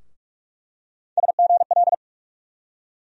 Morse code, heard as a keyed tone, about 46 wpm, 700 Hz. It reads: SGL
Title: USA Call Sign Postfixes - 3 Letters 45wpm